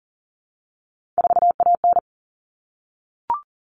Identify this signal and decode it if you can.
Morse code, heard as a keyed tone, about 40 wpm, 700 Hz. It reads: 4AN
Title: All Cut Numbers - 3-Digit 40wpm